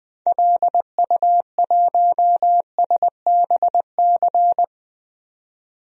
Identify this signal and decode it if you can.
Morse code, heard as a keyed tone, about 20 wpm, 700 Hz. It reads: LU1SBC
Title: DXCC-100 Call Signs - 6 Characters - 20wpm